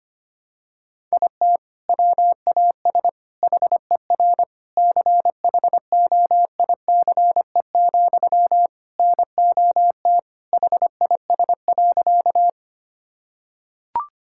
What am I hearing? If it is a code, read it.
Morse code, heard as a keyed tone, about 25 wpm, 700 Hz. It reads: IT WAS HER CHOICE, NOT HIS.
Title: Sentences from Top 1000 Words - Review Part 6 - 25wpm